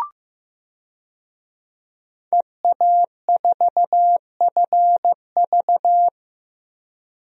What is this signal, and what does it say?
Morse code, heard as a keyed tone, about 15 wpm, 700 Hz. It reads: EA4FV
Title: DXCC-100 Call Signs - 5 Characters - Repeated 2x - 15wpm